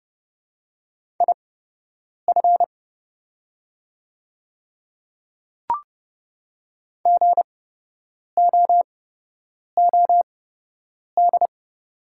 Morse code, heard as a keyed tone, about 30 wpm, spaced as for 8 wpm, 690 Hz. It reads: IF GOOD
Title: 8x Character Spacing - Top 200 Words - 30wpm